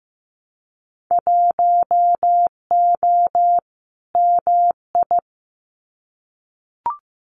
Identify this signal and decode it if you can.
Morse code, heard as a keyed tone, about 15 wpm, 700 Hz. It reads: 1O MI
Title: Winter Field Day Exchanges Encore 15wpm